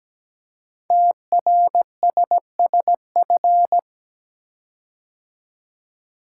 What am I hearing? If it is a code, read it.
Morse code, heard as a keyed tone, about 17 wpm, 700 Hz. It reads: TRSSF
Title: Code Groups - 5 Letters - 17wpm